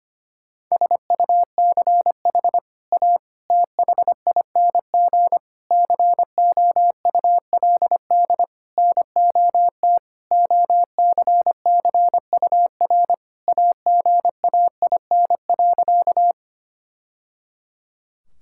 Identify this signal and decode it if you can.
Morse code, heard as a keyed tone, about 25 wpm, 700 Hz. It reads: SUCH A THING COULD NOT OCCUR AGAIN.
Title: Sentences from Top 1000 Words - Review Part 6 - 25wpm